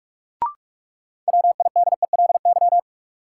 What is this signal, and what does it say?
Morse code, heard as a keyed tone, about 46 wpm, 700 Hz. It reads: WIDELY
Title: Sentences from Top 1200 Words - New Word Drill Alt - 45wpm